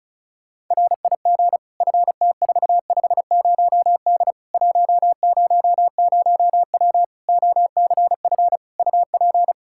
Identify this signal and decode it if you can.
Morse code, heard as a keyed tone, about 36 wpm, 700 Hz. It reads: RIG FT450D 100W OCF UP
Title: Practice QSO - Exchange 2 - Info - 35wpm